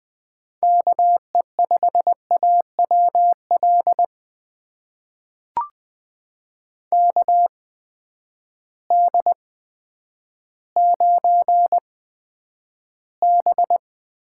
Morse code, heard as a keyed tone, about 20 wpm, spaced as for 10 wpm, 700 Hz. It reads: KE5AWL KD9B
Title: USA Call Signs - 6 Characters - 8x Character Spacing - 20wpm